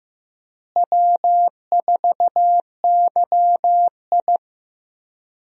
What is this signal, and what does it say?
Morse code, heard as a keyed tone, about 15 wpm, 700 Hz. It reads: W4YI
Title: USA Call Signs - 4 Characters - Repeated 2x - 15wpm